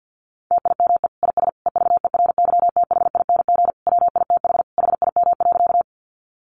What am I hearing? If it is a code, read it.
Morse code, heard as a keyed tone, about 50 wpm, 700 Hz. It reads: TIME IS EVERYTHING WITH HIM.
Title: Sentences from Top 700 Words 50wpm